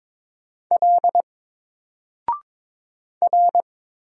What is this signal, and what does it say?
Morse code, heard as a keyed tone, about 22 wpm, 700 Hz. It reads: L R
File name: Single-Letter-Number-22wpm